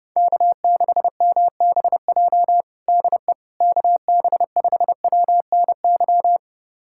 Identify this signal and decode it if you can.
Morse code, heard as a keyed tone, about 32 wpm, 700 Hz. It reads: K6MBJ DE KB5WNY
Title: USA Callsign DE Callsign 30wpm